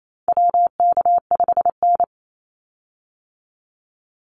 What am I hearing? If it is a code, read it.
Morse code, heard as a keyed tone, about 28 wpm, 700 Hz. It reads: WK5N
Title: Call Signs - 28wpm